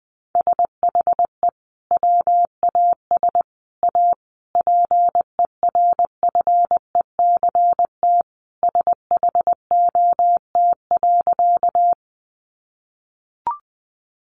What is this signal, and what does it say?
Morse code, heard as a keyed tone, about 20 wpm, 700 Hz. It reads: SHE WAS A PERFECT SHOT.
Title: Sentences from Top 1100 Words 20wpm